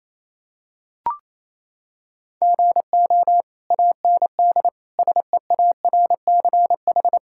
Morse code, heard as a keyed tone, about 28 wpm, 700 Hz. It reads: GO AND SEARCH
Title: Sentences from Top 1000 Words - Review Part 5 - 28wpm